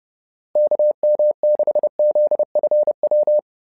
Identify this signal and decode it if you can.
Morse code, heard as a keyed tone, about 30 wpm, 600 Hz. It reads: KM6ZFW
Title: SOTA QSO Practice 30wpm